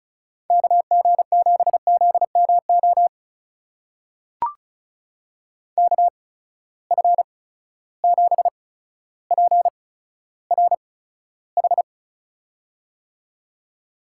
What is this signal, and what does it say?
Morse code, heard as a keyed tone, about 36 wpm, spaced as for 15 wpm, 700 Hz. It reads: KG7ZMO KF7PRH
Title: USA Call Signs - 6 Characters - 8x Character Spacing - 35wpm